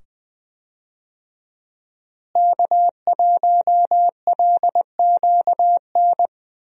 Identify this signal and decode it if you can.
Morse code, heard as a keyed tone, about 20 wpm, 700 Hz. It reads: K1LQN
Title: USA Call Signs - 5 Characters - Repeated 2x - 20wpm